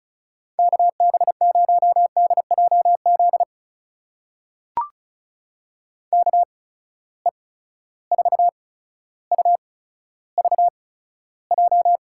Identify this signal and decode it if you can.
Morse code, heard as a keyed tone, about 34 wpm, spaced as for 15 wpm, 700 Hz. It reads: KB0DJZ KE4UVJ
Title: USA Call Signs - 6 Characters - 8x Character Spacing - 35wpm